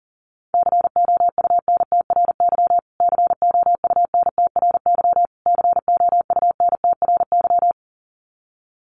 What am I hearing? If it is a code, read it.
Morse code, heard as a keyed tone, about 40 wpm, 700 Hz. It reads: COUNTRY COUNTRY COUNTRY
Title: Top 500 Words Repeated 40wpm